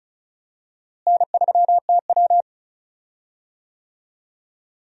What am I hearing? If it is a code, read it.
Morse code, heard as a keyed tone, about 36 wpm, 700 Hz. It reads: N3TW
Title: USA Call Signs - 4 Characters - Repeated 3x - 35wpm